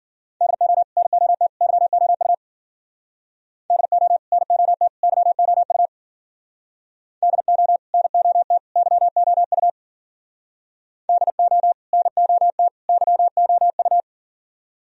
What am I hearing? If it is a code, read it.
Morse code, heard as a keyed tone, about 53 wpm, 700 Hz. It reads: DO NOT YOU DO NOT YOU DO NOT YOU DO NOT YOU